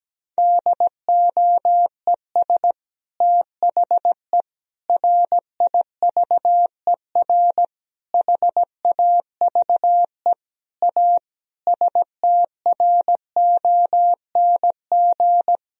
Morse code, heard as a keyed tone, about 17 wpm, 700 Hz. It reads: DOES THE RIVER HAVE A STRONG
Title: Sentences from Top 1000 Words - Review Part 4 - 17wpm